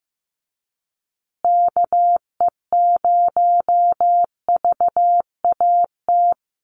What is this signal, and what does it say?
Morse code, heard as a keyed tone, about 15 wpm, 700 Hz. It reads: KE0VAT
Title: USA Call Signs Repeated 2x 15wpm